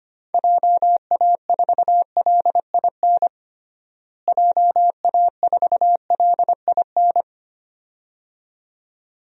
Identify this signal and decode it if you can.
Morse code, heard as a keyed tone, about 25 wpm, 700 Hz. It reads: JA4LIN JA4LIN
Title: DXCC-100 Call Signs - 6 Characters - Repeated 2x - 25wpm